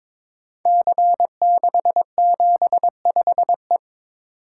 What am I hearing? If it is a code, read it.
Morse code, heard as a keyed tone, about 22 wpm, 700 Hz. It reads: C675E
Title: Code Groups - 5 Letter:Numbers - 22wpm